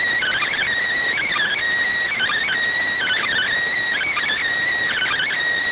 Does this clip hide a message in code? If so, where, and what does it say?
Yes: Emergency Alert System (SAME) tones from 1.1 to 5.3 s, heard as harsh buzzing bursts.